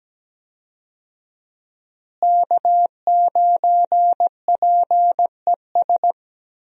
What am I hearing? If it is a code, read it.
Morse code, heard as a keyed tone, about 17 wpm, 700 Hz. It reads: K9PES
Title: USA Call Signs - 5 Characters - Repeated 2x - 17wpm